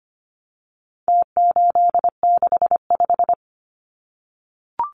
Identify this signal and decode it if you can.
Morse code, heard as a keyed tone, about 25 wpm, 700 Hz. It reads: T865
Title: Numbers - Contest Style - 4-Digit 25wpm